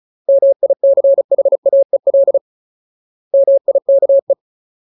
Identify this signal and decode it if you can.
Morse code, heard as a keyed tone, about 36 wpm, 550 Hz. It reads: MICHAEL MIKE
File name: Ultimate Word List / 35WPM Ultimate